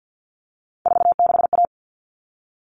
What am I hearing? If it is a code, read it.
Morse code, heard as a keyed tone, about 50 wpm, 700 Hz. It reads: 46A
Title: All Cut Numbers - 3-Digit 50wpm